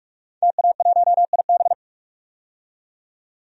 Morse code, heard as a keyed tone, about 44 wpm, 690 Hz. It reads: TA1IB
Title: DXCC-100 Call Signs - 5 Characters - 45wpm